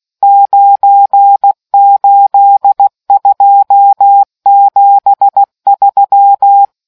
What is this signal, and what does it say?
Morse code, heard as a keyed tone, about 16 wpm, 800 Hz. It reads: 98273